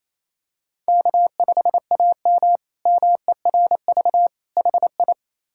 Morse code, heard as a keyed tone, about 27 wpm, 700 Hz. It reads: K5AM MERV HI
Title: K1USN SST QSO Practice - 28wpm